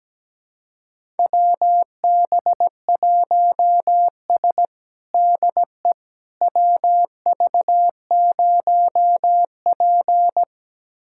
Morse code, heard as a keyed tone, about 17 wpm, 700 Hz. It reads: WB1S DE WV0P
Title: USA Callsign DE Callsign - 4 Characters - 17wpm